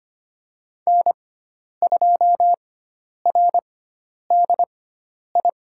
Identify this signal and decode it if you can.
Morse code, heard as a keyed tone, about 25 wpm, spaced as for 10 wpm, 700 Hz. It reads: N2RDI
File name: Lesson 16g - Letter c - Call Signs 25wpm